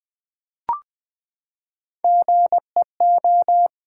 Morse code, heard as a keyed tone, about 20 wpm, 700 Hz. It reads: GEO
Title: Build A Word - 3 Letters At A Time - 9-Letter Words - 20wpm